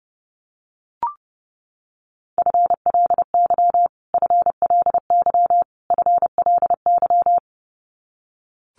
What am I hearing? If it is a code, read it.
Morse code, heard as a keyed tone, about 30 wpm, 700 Hz. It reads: FLY FLY FLY
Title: Top 500 Words Repeated 30wpm